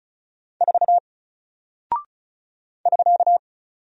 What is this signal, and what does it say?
Morse code, heard as a keyed tone, about 34 wpm, 700 Hz. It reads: 4 <SK>
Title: Single Character (Letter:Number:Punctuation:Prosign) Alt 35wpm